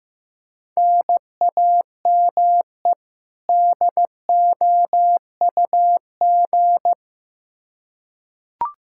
Morse code, heard as a keyed tone, about 15 wpm, 700 Hz. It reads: NAME DOUG
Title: Practice QSO - Exchange 1 - Beginning - 15wpm